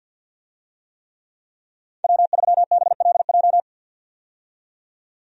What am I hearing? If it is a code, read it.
Morse code, heard as a keyed tone, about 51 wpm, 700 Hz. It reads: W3BLJ
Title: USA Call Signs - 5 Characters - 50wpm